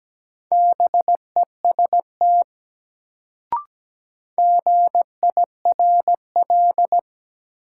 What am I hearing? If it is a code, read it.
Morse code, heard as a keyed tone, about 17 wpm, 700 Hz. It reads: BEST GIRL
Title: Sentences from Top 400 Words - New Word Drill - 17wpm